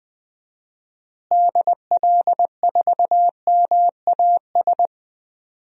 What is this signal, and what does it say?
Morse code, heard as a keyed tone, about 20 wpm, 700 Hz. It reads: DL4MAS
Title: DXCC-100 Call Signs - 6 Characters - Repeated 3x - 20wpm